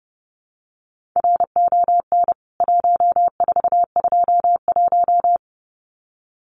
Morse code, heard as a keyed tone, about 30 wpm, 700 Hz. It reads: RON 1421
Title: CWT Exchange 30wpm